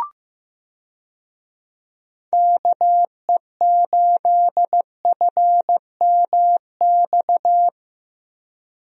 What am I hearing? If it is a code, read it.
Morse code, heard as a keyed tone, about 15 wpm, 700 Hz. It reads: KE8FMX
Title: USA Call Signs - 6 Characters - Repeated 3x - 15wpm